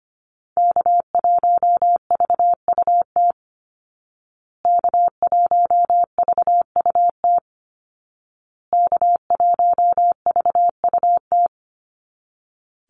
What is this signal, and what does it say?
Morse code, heard as a keyed tone, about 25 wpm, 700 Hz. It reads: K1VUT K1VUT K1VUT
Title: Call Signs Repeated 2x - 25wpm